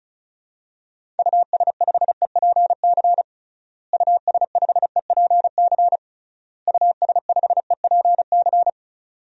Morse code, heard as a keyed tone, about 36 wpm, 700 Hz. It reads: US5EPC US5EPC US5EPC
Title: DXCC-100 Call Signs - 6 Characters - Repeated 3x - 35wpm